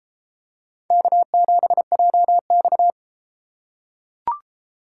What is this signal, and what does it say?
Morse code, heard as a keyed tone, about 29 wpm, 700 Hz. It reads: K7JX